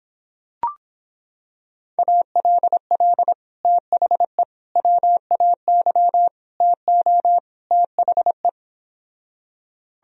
Morse code, heard as a keyed tone, about 26 wpm, 700 Hz. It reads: ALL THE WAY TO THE